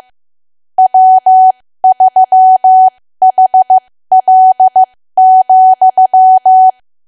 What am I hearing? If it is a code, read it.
Morse code, heard as a keyed tone, about 15 wpm, spaced as for 13 wpm, 750 Hz. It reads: W3HL,
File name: cwa / W1AW-Level2 / 13WPM